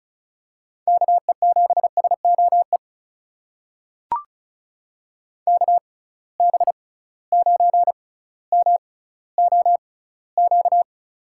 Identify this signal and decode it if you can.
Morse code, heard as a keyed tone, about 36 wpm, spaced as for 18 wpm, 700 Hz. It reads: KE7SOE KB9MOQ
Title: USA Call Signs - 6 Characters - 6x Character Spacing - 35wpm